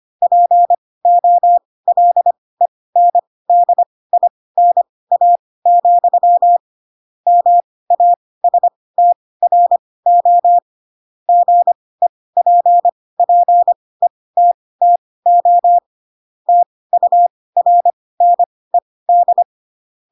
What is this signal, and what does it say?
Morse code, heard as a keyed tone, about 25 wpm, spaced as for 18 wpm, 700 Hz. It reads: POLENDINA, MASTRO GEPPETTO TURNED
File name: Pinocchio / Pinocchio ch2 12 25 ik1hge